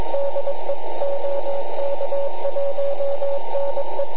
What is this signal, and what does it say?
Morse code, heard as a keyed tone, about 21 wpm, 560 Hz, with radio noise. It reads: DE OK1NE
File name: ARCHIV / QRP2005 / ok1ne